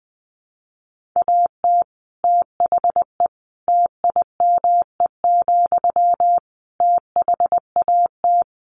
Morse code, heard as a keyed tone, about 19 wpm, 700 Hz. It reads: AT THE TIME, THAT
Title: Sentences from Top 1400 Words 20wpm